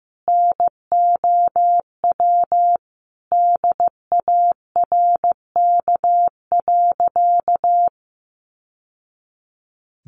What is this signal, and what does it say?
Morse code, heard as a keyed tone, about 15 wpm, 700 Hz. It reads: NOW DARK.
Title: Sentences from Top 700 Words 15wpm